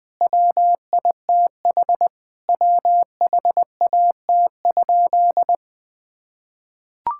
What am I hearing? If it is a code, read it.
Morse code, heard as a keyed tone, about 20 wpm, 700 Hz. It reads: WITH WHAT?
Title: Sentences from Top 1600 Words 20wpm